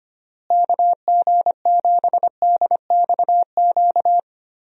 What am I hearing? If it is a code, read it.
Morse code, heard as a keyed tone, about 25 wpm, 700 Hz. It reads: KG7DXQ
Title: USA Call Signs - 6 Characters - Repeated 2x - 25wpm